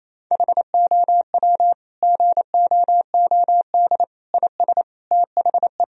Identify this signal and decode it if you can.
Morse code, heard as a keyed tone, about 28 wpm, 700 Hz. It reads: HOW GOOD IS THE
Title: Sentences from Top 1000 Words - Review Part 4 - 28wpm